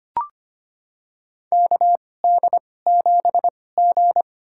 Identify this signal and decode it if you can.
Morse code, heard as a keyed tone, about 25 wpm, spaced as for 18 wpm, 700 Hz. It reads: KD7G
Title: USA Call Signs - 4 Characters - 2x Character Spacing - 25wpm